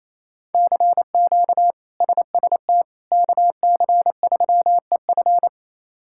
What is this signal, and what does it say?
Morse code, heard as a keyed tone, about 28 wpm, 700 Hz. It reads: CQ SST KC3EF
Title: K1USN SST QSO Practice - 28wpm